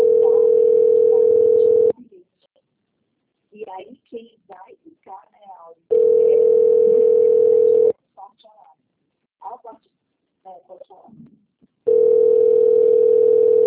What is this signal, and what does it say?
Ringback tone (North America), 440 + 480 Hz, with speech.
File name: rg-1600-unknown- - - .21469